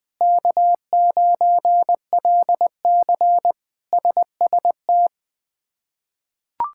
Morse code, heard as a keyed tone, about 20 wpm, 700 Hz. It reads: K9LC SST
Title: K1USN SST QSO Practice - 20wpm